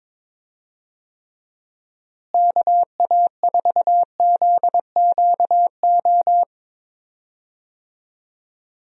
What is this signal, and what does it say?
Morse code, heard as a keyed tone, about 22 wpm, 700 Hz. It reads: KA4ZQO
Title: USA Call Signs - 6 Characters - Repeated 3x - 22wpm